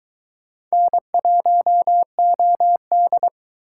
Morse code, heard as a keyed tone, about 24 wpm, 700 Hz. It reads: N1OD